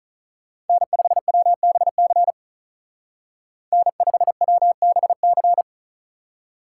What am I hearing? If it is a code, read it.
Morse code, heard as a keyed tone, about 46 wpm, 700 Hz. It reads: N5WBC N5WBC